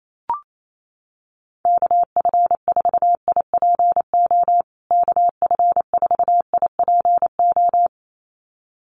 Morse code, heard as a keyed tone, about 28 wpm, 700 Hz. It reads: KF4IPO KF4IPO
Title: USA Call Signs Repeated 2x 28wpm